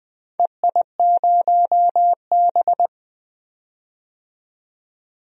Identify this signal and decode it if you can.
Morse code, heard as a keyed tone, about 20 wpm, 700 Hz. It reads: EI0B
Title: DXCC-100 Call Signs - 4 Characters - 20wpm